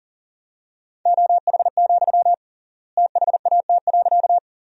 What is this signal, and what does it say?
Morse code, heard as a keyed tone, about 40 wpm, 700 Hz. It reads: OH, THAT.
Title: 3x Word-Spacing - Sentences from Top 500 Words - 40wpm